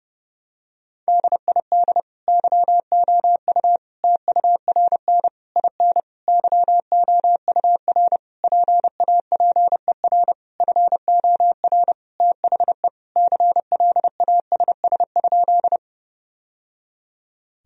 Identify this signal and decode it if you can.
Morse code, heard as a keyed tone, about 30 wpm, 700 Hz. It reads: DID YOU TURN IN YOUR PAPER FOR THE CLASS?
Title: Sentences from Top 1000 Words - Review Part 2 - 30wpm